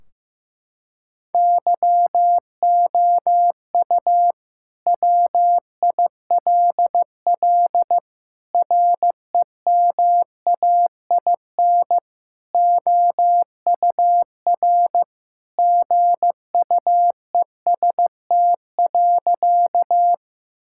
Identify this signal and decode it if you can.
Morse code, heard as a keyed tone, about 15 wpm, 700 Hz. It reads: YOU WILL REMAIN OUR GUEST.
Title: Sentences from Top 1600 Words 15wpm